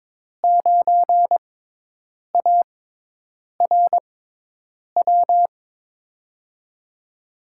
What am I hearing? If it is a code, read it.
Morse code, heard as a keyed tone, about 22 wpm, spaced as for 8 wpm, 700 Hz. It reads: 9ARW
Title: USA Call Signs - 5 Characters - 6x Character Spacing - 22wpm